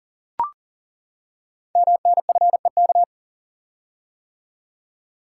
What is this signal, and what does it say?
Morse code, heard as a keyed tone, about 42 wpm, 690 Hz. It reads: MNFEK
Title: Code Groups - 5 Letters - 40wpm